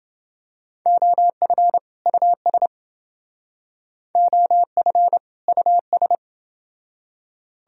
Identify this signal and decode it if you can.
Morse code, heard as a keyed tone, about 30 wpm, 700 Hz. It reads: OF US OF US